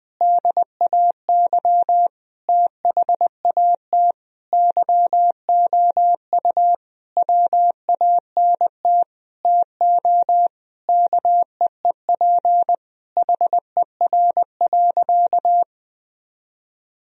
Morse code, heard as a keyed tone, about 21 wpm, 700 Hz. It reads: DAY THAT YOU WANT TO KEEP HER.
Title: Sentences from Top 1000 Words - Review Part 2 - 20wpm